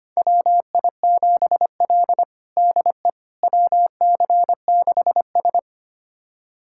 Morse code, heard as a keyed tone, about 25 wpm, 700 Hz. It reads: WI7L DE WC6S
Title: USA Callsign DE Callsign - 4 Characters - 25wpm